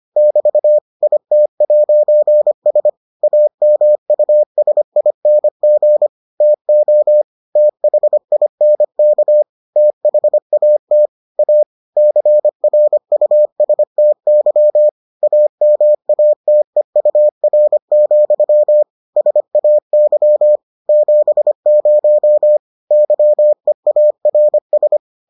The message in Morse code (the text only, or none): <BT> IT'S AMUSING TO THINK THAT A CRUSTY AMATEUR, SAY 70 YEARS